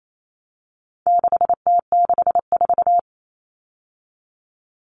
Morse code, heard as a keyed tone, about 28 wpm, 700 Hz. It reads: 6T64
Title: All Cut Numbers - 4-Digit 28wpm